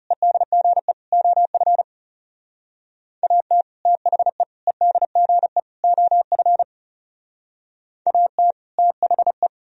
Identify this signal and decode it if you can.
Morse code, heard as a keyed tone, about 38 wpm, 700 Hz. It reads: EDGE OF AT THE EDGE OF AT THE
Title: Speed-Racing - Sets of 4 Words 30wpm